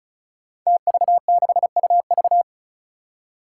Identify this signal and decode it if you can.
Morse code, heard as a keyed tone, about 36 wpm, 700 Hz. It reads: TV6UV
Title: Five Challenging Letter-Number Combinations 35wpm